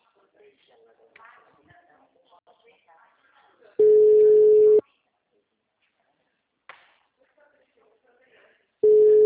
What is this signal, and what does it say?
Ringback tone (Europe, 425 Hz), with speech.